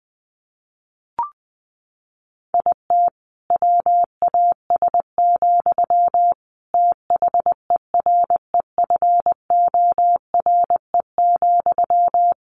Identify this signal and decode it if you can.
Morse code, heard as a keyed tone, about 20 wpm, 700 Hz. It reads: IT WAS, THEREFORE,
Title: Sentences from Top 1500 Words 20wpm